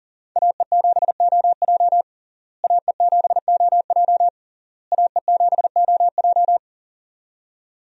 Morse code, heard as a keyed tone, about 40 wpm, 700 Hz. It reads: AE7OJ AE7OJ AE7OJ
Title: USA Call Signs - 5 Characters - Repeated 3x - 40wpm